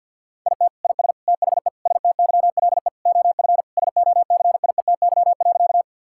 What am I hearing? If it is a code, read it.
Morse code, heard as a keyed tone, about 51 wpm, 690 Hz. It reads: IT IS THE STYLE OF SOCIETY.
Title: Sentences from Top 1000 Words - Review Part 5 - 50wpm